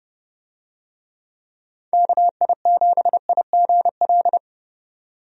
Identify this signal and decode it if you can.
Morse code, heard as a keyed tone, about 30 wpm, 700 Hz. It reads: KI7IGL
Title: USA Call Signs - 6 Characters - 30wpm